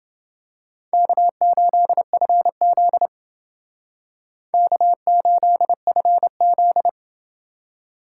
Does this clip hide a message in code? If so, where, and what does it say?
Yes: Morse code from 0.9 to 6.9 s, heard as a keyed tone, about 28 wpm, 700 Hz. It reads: K8FZ K8FZ